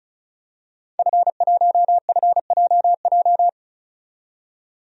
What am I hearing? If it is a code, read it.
Morse code, heard as a keyed tone, about 34 wpm, 700 Hz. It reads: F1FJJ
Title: DXCC-100 Call Signs - 5 Characters - 35wpm